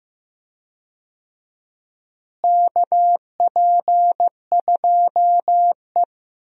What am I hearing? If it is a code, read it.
Morse code, heard as a keyed tone, about 15 wpm, 700 Hz. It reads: KP2E
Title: DXCC-100 Call Signs - 4 Characters - Repeated 3x - 15wpm